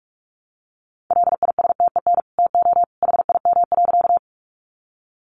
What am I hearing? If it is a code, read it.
Morse code, heard as a keyed tone, about 44 wpm, 700 Hz. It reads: LISTEN TO HIM.
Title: Sentences from Top 400 Words 45wpm